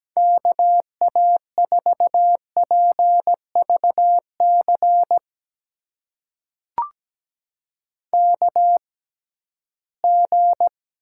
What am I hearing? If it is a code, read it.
Morse code, heard as a keyed tone, about 17 wpm, spaced as for 13 wpm, 700 Hz. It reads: KA4PVC KG
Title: USA Call Signs - 6 Characters - 6x Character Spacing - 17wpm